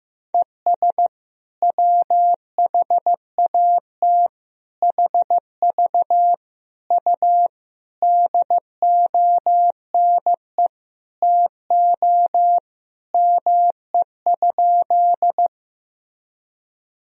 Morse code, heard as a keyed tone, about 15 wpm, 700 Hz. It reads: ES WHAT HV U DONE TO ME?
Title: Sentences from Top 500 Words and Abbreviations 15wpm